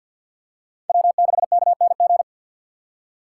Morse code, heard as a keyed tone, about 52 wpm, 700 Hz. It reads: W6KNG
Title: USA Call Signs - 5 Characters - 6x Character Spacing - 50wpm